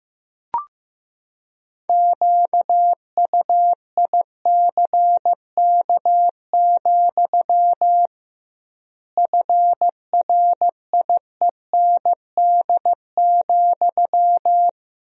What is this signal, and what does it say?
Morse code, heard as a keyed tone, about 15 wpm, 700 Hz. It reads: QUICK, FRIEND,
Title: 2x Word-Spacing - Sentences from Top 500 Words - 15wpm